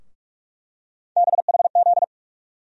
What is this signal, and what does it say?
Morse code, heard as a keyed tone, about 45 wpm, 700 Hz. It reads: BHZ
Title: USA Call Sign Postfixes - 3 Letters 45wpm